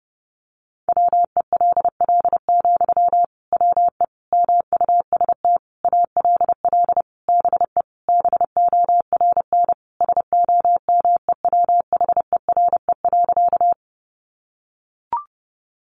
Morse code, heard as a keyed tone, about 30 wpm, 700 Hz. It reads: WELL, WE MUST ALL BE BORN SOMEWHERE.
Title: Sentences from Top 1400 Words 30wpm